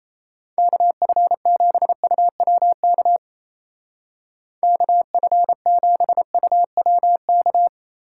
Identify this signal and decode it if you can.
Morse code, heard as a keyed tone, about 35 wpm, 700 Hz. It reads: KF7UWK KF7UWK
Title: Speed Racing - USA 2x3 Call Signs 28wpm